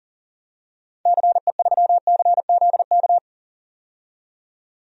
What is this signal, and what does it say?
Morse code, heard as a keyed tone, about 37 wpm, 700 Hz. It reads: CE3CZK
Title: DXCC-100 Call Signs - 6 Characters - 40wpm